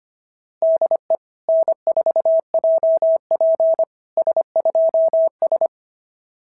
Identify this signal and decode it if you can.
Morse code, heard as a keyed tone, about 25 wpm, 660 Hz. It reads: DE N4JP S2S
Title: SOTA QSO Practice 25wpm